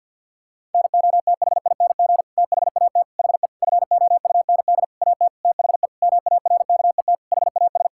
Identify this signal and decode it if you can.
Morse code, heard as a keyed tone, about 48 wpm, 700 Hz. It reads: NOTHING THAT HE FOUND AT THE MARKET HAS
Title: Sentences from Top 1000 Words - Review Part 6 - 50wpm